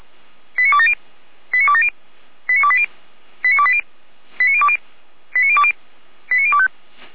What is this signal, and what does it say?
Five-tone selective call (ZVEI), heard as quick runs of five tones, tones of about 70 ms. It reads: 89270 89270 89280 89280 89020 89020 89025